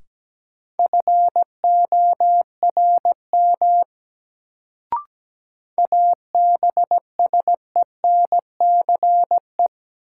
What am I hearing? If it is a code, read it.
Morse code, heard as a keyed tone, about 17 wpm, 700 Hz. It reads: FORM ABSENCE
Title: Sets of 4 Words - New Word Drill - 17wpm